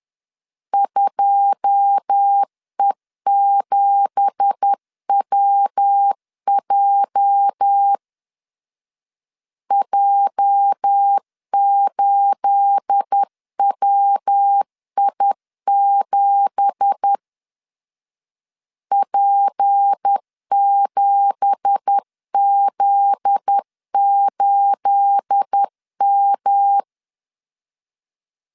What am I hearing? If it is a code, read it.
Morse code, heard as a keyed tone, about 10 wpm, 800 Hz. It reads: 2E7WJ J8WI7 P7Z8M
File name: LEZIONI / 9 2867 10wpm 2733